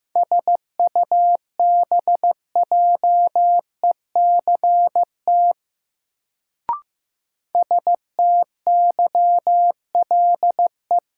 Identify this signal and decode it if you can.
Morse code, heard as a keyed tone, about 15 wpm, 700 Hz. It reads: SUBJECT STYLE
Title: Sentences from Top 800 Words - New Word Drill - 15wpm